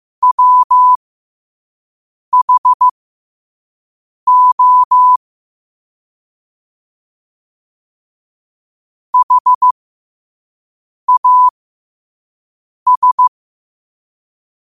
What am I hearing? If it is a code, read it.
Morse code, heard as a keyed tone, about 15 wpm, spaced as for 5 wpm, 1000 Hz. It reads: WHO HAS
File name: qotdfiles / 5wpm